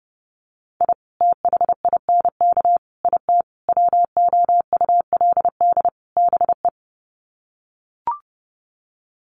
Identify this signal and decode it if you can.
Morse code, heard as a keyed tone, about 30 wpm, 700 Hz. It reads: I THINK IT WOULD BE
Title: Sets of 5 Words 30wpm